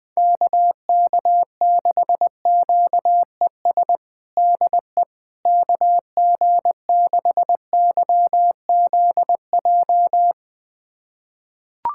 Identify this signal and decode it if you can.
Morse code, heard as a keyed tone, about 20 wpm, 700 Hz. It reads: KK6QES DE KG6YZJ
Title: USA Callsign DE Callsign - 6 Characters - 20wpm